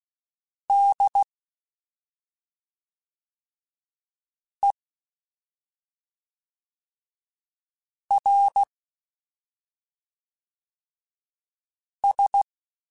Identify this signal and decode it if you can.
Morse code, heard as a keyed tone, about 16 wpm, 800 Hz. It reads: D E R S
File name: Special INGs / Special INGs-mp3s / SlowToFaster ing @5